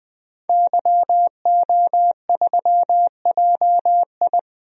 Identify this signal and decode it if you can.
Morse code, heard as a keyed tone, about 20 wpm, 700 Hz. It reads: YO3JI
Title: DXCC-100 Call Signs - 5 Characters - 20wpm